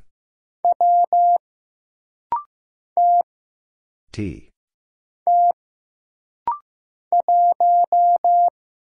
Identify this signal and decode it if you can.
Morse code, heard as a keyed tone, about 15 wpm, 700 Hz. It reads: W T T 1
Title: Single Letter-Number Alt - Rapid-Fire 15wpm